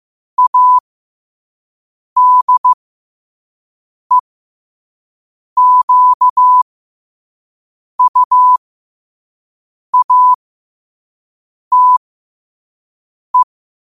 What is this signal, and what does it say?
Morse code, heard as a keyed tone, about 15 wpm, spaced as for 5 wpm, 1000 Hz. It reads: ADEQUATE
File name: 5wpm